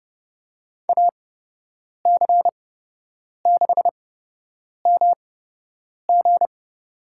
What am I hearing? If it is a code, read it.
Morse code, heard as a keyed tone, about 32 wpm, spaced as for 8 wpm, 700 Hz. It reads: AC6MG
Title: USA Call Signs - 5 Characters - 8x Character Spacing - 30wpm